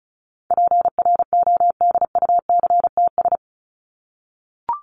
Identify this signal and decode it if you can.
Morse code, heard as a keyed tone, about 34 wpm, 700 Hz. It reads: PRODUCTS
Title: ES:S Suffix - Top 100 Words with Base Words 35wpm